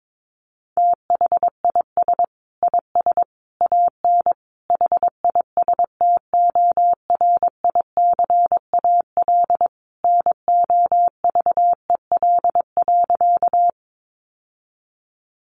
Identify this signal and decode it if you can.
Morse code, heard as a keyed tone, about 22 wpm, 700 Hz. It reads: THIS IS AN HISTORICAL NOVEL.
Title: Sentences from Top 1300 Words 22wpm